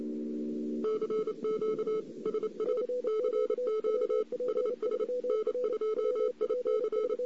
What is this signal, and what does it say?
Morse code, heard as a keyed tone, about 28 wpm, 430 Hz. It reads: CQ SS CQ SS N2IC
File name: CW / 2008-ss-cw-k5tr-op-n5rz -02:00